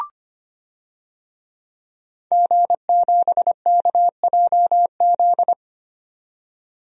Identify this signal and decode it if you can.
Morse code, heard as a keyed tone, about 25 wpm, 700 Hz. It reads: G7KJZ
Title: DXCC-100 Call Signs - 5 Characters - 25wpm